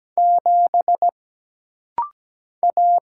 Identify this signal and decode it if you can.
Morse code, heard as a keyed tone, about 17 wpm, 700 Hz. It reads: 7 A
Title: Single Letter-Number - Rapid-Fire 17wpm